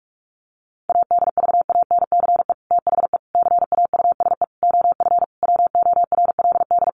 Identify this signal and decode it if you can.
Morse code, heard as a keyed tone, about 44 wpm, 700 Hz. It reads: ADVANCE THE CAUSE OF WORLD
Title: Sentences from Top 1400 Words 45wpm